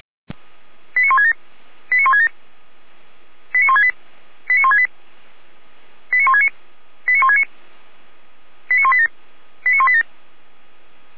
Five-tone selective call (ZVEI), heard as quick runs of five tones, tones of about 80 ms. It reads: 89167 89167 89168 89168 89169 89169 89187 89187
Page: /455/SDR/ZVEI/89121/